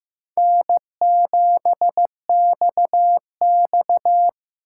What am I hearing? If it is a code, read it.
Morse code, heard as a keyed tone, about 18 wpm, 700 Hz. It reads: N7XX
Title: Speed Racing - USA 1x2 Call Signs 15wpm